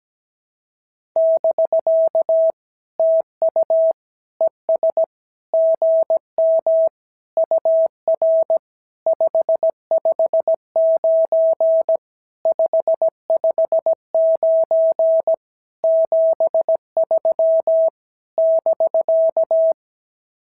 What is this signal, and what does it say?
Morse code, heard as a keyed tone, about 16 wpm, 650 Hz. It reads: <BK> TU ES GM UR 559 559 73 <BK>
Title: SOTA QSO Practice 17wpm